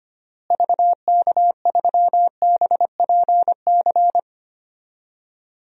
Morse code, heard as a keyed tone, about 25 wpm, 700 Hz. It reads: VK3BPC
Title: DXCC-100 Call Signs - 6 Characters - Repeated 3x - 25wpm